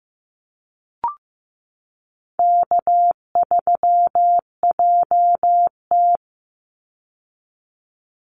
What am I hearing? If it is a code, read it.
Morse code, heard as a keyed tone, about 15 wpm, 700 Hz. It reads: K3JT
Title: Easy Call Signs - 15wpm